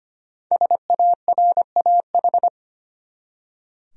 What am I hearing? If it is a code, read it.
Morse code, heard as a keyed tone, about 25 wpm, 700 Hz. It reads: SARAH
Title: USA Common Names - 25wpm